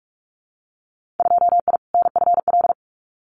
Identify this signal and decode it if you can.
Morse code, heard as a keyed tone, about 46 wpm, 700 Hz. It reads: 2I NFL
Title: Winter Field Day Exchanges 45wpm